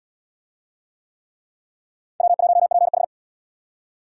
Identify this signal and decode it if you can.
Morse code, heard as a keyed tone, about 68 wpm, 700 Hz. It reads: W1PH